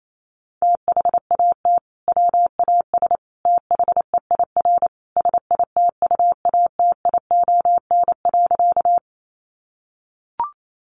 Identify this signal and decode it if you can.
Morse code, heard as a keyed tone, about 28 wpm, 700 Hz. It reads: THAT WAS THEIR SITUATION.
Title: Sentences from Top 1200 Words 28wpm